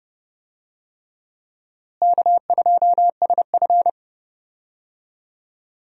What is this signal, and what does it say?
Morse code, heard as a keyed tone, about 29 wpm, 700 Hz. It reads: K2SF
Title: USA Call Signs - 4 Characters - Repeated 3x - 30wpm